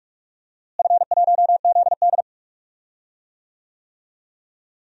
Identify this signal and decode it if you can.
Morse code, heard as a keyed tone, about 46 wpm, 700 Hz. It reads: F1ZD
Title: DXCC-100 Call Signs - 4 Characters - Repeated 2x - 45wpm